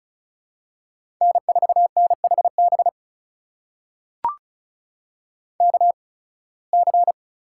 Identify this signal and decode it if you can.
Morse code, heard as a keyed tone, about 36 wpm, spaced as for 23 wpm, 700 Hz. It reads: N4NHB KC
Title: USA Call Signs - 8x Character Spacing - 35wpm